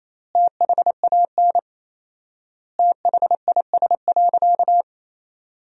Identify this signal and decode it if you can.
Morse code, heard as a keyed tone, about 28 wpm, 700 Hz. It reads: THAN THIS.
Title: 4x Word-Spacing - Sentences from Top 100 Words - 28wpm